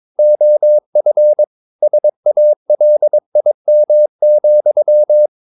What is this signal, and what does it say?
Morse code, heard as a keyed tone, about 22 wpm, 600 Hz. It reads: OF SALIM,
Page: Romanen Never af Ken Follet på engelsk.